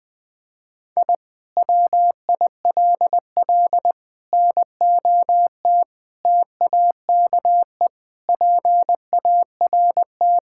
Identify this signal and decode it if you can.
Morse code, heard as a keyed tone, about 20 wpm, 700 Hz. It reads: I WILL NOT TAKE PART
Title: Sentences from Top 1000 Words - Review Part 6 - 20wpm